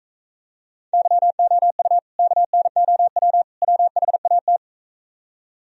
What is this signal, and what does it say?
Morse code, heard as a keyed tone, about 43 wpm, 700 Hz. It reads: YOU KNOW WHAT